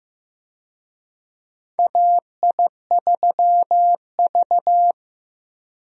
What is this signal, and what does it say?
Morse code, heard as a keyed tone, about 15 wpm, 700 Hz. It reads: AI3V
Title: USA Call Signs - 4 Characters - Repeated 2x - 15wpm